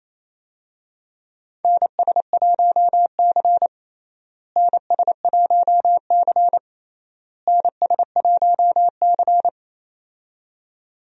Morse code, heard as a keyed tone, about 28 wpm, 700 Hz. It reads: NS1C NS1C NS1C
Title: USA Call Signs - 4 Characters - Repeated 3x - 28wpm